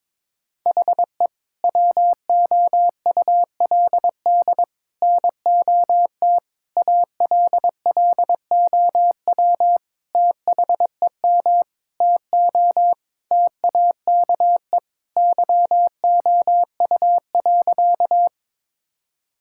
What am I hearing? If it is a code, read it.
Morse code, heard as a keyed tone, about 22 wpm, 700 Hz. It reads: HE WOULD NOT ALLOW THEM TO TAKE YOU.
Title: Sentences from Top 1000 Words - Review Part 5 - 22wpm